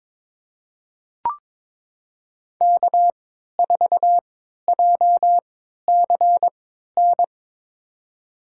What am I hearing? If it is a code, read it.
Morse code, heard as a keyed tone, about 23 wpm, spaced as for 13 wpm, 700 Hz. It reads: K4JCN
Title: USA Call Signs - 5 Characters - 3x Character Spacing - 22wpm